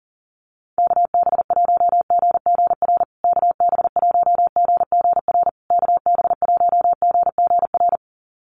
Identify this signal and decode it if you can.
Morse code, heard as a keyed tone, about 37 wpm, 700 Hz. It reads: KB1GGR KB1GGR KB1GGR
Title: USA Call Signs Repeated 3x 40wpm